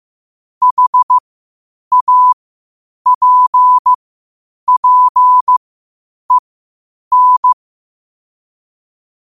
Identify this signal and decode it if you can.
Morse code, heard as a keyed tone, about 15 wpm, spaced as for 8 wpm, 1000 Hz. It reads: HAPPEN
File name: qotdfiles / 8wpm